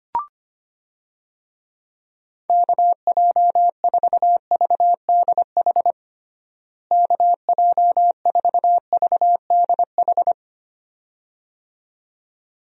Morse code, heard as a keyed tone, about 25 wpm, 700 Hz. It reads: KJ4VDH KJ4VDH
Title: USA Call Signs - 6 Characters - Repeated 2x - 25wpm